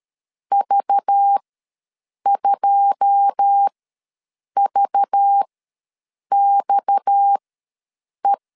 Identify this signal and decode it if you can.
Morse code, heard as a keyed tone, about 13 wpm, spaced as for 7 wpm, 800 Hz. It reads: V2VXE
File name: LEZIONI / 9 2 ws 12wpm 3406